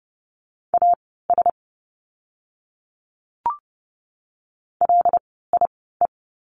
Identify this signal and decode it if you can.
Morse code, heard as a keyed tone, about 30 wpm, spaced as for 17 wpm, 700 Hz. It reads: AS LIE
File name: Lesson 16f - Letter c - Words 30wpm